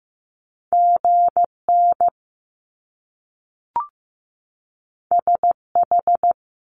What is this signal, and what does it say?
Morse code, heard as a keyed tone, about 15 wpm, 700 Hz. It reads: GN SH
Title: Top Two Letter Combinations in Words 15wpm